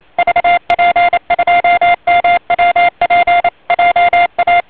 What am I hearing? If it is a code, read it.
Morse code, heard as a keyed tone, about 28 wpm, 700 Hz, with noise. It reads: VP2MWP JA